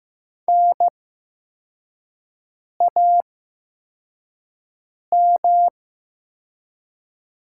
Morse code, heard as a keyed tone, about 15 wpm, spaced as for 4 wpm, 700 Hz. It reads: NAM
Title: 8x Character Spacing - Top 200 Words - 15wpm